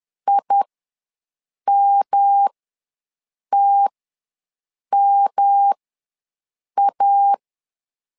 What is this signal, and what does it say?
Morse code, heard as a keyed tone, about 10 wpm, spaced as for 6 wpm, 800 Hz. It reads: IMTMA
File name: LEZIONI / 3 MWRO / 3 M ws 10wpm 2121